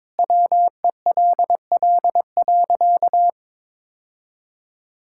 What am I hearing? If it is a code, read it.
Morse code, heard as a keyed tone, about 22 wpm, 700 Hz. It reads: WELL.
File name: practice / 22wpm / Sentences with 2-Word Phrases - 22wpm